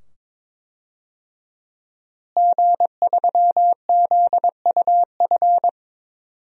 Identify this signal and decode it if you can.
Morse code, heard as a keyed tone, about 22 wpm, 700 Hz. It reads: G3ZUF
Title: DXCC-100 Call Signs - 5 Characters - Repeated 2x - 22wpm